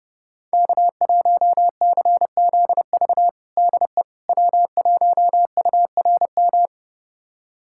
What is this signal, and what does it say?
Morse code, heard as a keyed tone, about 30 wpm, 700 Hz. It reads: K1CZV DE W1URM
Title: USA Callsign DE Callsign - 5 Characters - 30wpm